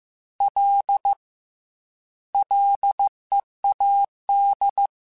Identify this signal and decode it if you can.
Morse code, heard as a keyed tone, about 15 wpm, 800 Hz. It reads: L LEAD
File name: Ls / Single L m3s / Single L Prefixe 3x2 Twice @13